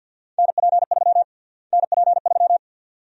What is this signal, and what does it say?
Morse code, heard as a keyed tone, about 51 wpm, 700 Hz. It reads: NP3 NP3
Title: USA Call Sign Prefixes - Repeated 2x - 50wpm